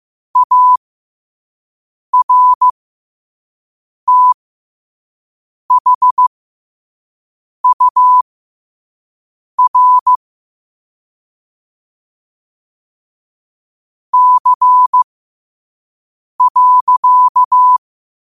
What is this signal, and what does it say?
Morse code, heard as a keyed tone, about 15 wpm, spaced as for 5 wpm, 1000 Hz. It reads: ARTHUR C.
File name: qotdfiles / 5wpm